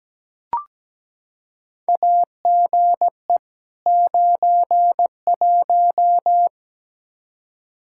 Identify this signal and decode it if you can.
Morse code, heard as a keyed tone, about 17 wpm, 700 Hz. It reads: AGE 91
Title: Practice QSO - Exchange 3 - More Info - 17wpm